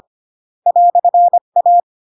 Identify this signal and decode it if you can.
Morse code, heard as a keyed tone, about 25 wpm, 700 Hz. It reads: "A
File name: Pinocchio / Pinocchio ch1 20 25 ik1hge